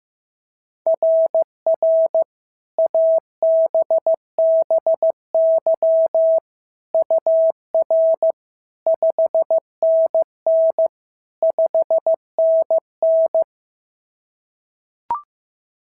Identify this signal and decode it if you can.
Morse code, heard as a keyed tone, about 16 wpm, 690 Hz. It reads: RR ABBY UR 5NN 5NN
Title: SOTA QSO Practice 15wpm